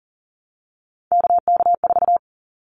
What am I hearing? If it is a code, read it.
Morse code, heard as a keyed tone, about 40 wpm, 700 Hz. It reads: KK4
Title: Callsign Prefixes for Top 25 DXCC - Repeated 3x - 40wpm